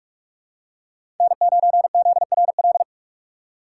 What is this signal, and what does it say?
Morse code, heard as a keyed tone, about 46 wpm, 700 Hz. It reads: N9ZRL
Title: USA Call Signs - 3x Character Spacing - 45wpm